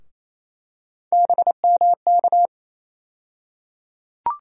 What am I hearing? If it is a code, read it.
Morse code, heard as a keyed tone, about 28 wpm, 700 Hz. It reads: BMK
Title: USA Call Sign Postfixes - 3 Letters 28wpm